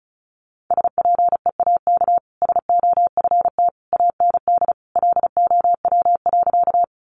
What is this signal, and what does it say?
Morse code, heard as a keyed tone, about 34 wpm, 700 Hz. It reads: SPEAK SOFT AND LOW.
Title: Sentences from Top 1500 Words 35wpm